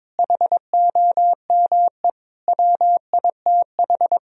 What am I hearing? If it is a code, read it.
Morse code, heard as a keyed tone, about 22 wpm, 700 Hz. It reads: HOME WITH
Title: Sentences from Top 1000 Words - Review Part 1 - 22wpm